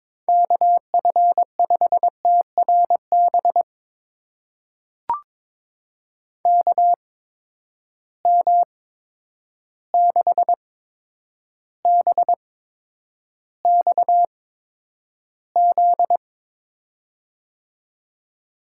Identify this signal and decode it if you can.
Morse code, heard as a keyed tone, about 23 wpm, spaced as for 10 wpm, 700 Hz. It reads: KF5TRB KM6BXZ
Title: USA Call Signs - 6 Characters - 8x Character Spacing - 22wpm